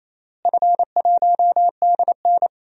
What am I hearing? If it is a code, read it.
Morse code, heard as a keyed tone, about 27 wpm, 700 Hz. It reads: F1DN
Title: DXCC-100 Call Signs - 4 Characters - Repeated 3x - 28wpm